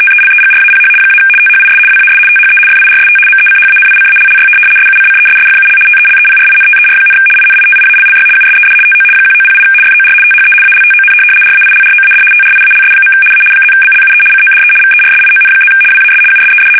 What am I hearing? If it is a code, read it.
RTTY, about 75 baud. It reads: TELEGRAPH - THIS WAS THE FIRST MECHANICALLY SUCCESSFUL TELEPRINTER, INITIALLY USED TO 1908 FOR THE ALTON RAILROAD TRIALS. A +BLUE CODE VERSION+ WAS USED
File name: fsk850-75B / teletype-corp